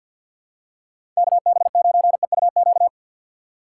Morse code, heard as a keyed tone, about 48 wpm, 690 Hz. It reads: KB9EUQ
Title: USA Call Signs - 6 Characters - 3x Character Spacing - 50wpm